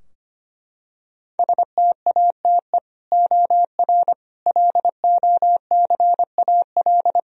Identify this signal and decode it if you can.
Morse code, heard as a keyed tone, about 25 wpm, 700 Hz. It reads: STATE OR LOCAL
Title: Sentences from Top 1000 Words - Review Part 3 - 25wpm